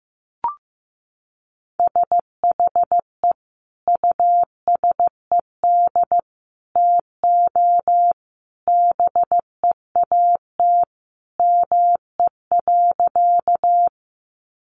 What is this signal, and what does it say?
Morse code, heard as a keyed tone, about 15 wpm, 700 Hz. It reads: SHE USED TO BEAT ME.
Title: Sentences from Top 1300 Words 15wpm